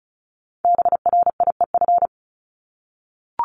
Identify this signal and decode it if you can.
Morse code, heard as a keyed tone, about 36 wpm, 700 Hz. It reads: BRIEF
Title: Multi-letter Phonemes Repeated 3x 35wpm